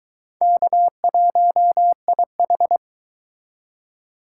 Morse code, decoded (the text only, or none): K1IH